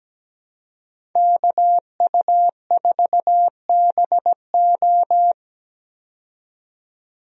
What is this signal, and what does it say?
Morse code, heard as a keyed tone, about 17 wpm, 700 Hz. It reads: KU4BO
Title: USA Call Signs - 5 Characters - 17wpm